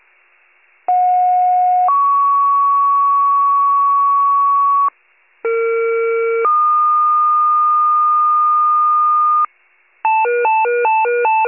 Two-tone page, heard as a one-second tone, then a three-second tone, about 730 then 1120 Hz, about 460 then 1220 Hz. It is heard from 0.9 to 9.5 s.